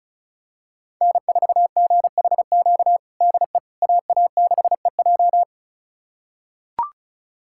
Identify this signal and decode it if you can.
Morse code, heard as a keyed tone, about 36 wpm, 700 Hz. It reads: N4GHQ DE AA6EJ
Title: USA Callsign DE Callsign - 5 Characters - 35wpm